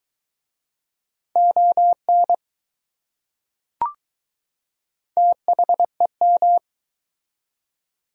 Morse code, heard as a keyed tone, about 20 wpm, 700 Hz. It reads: ON THEM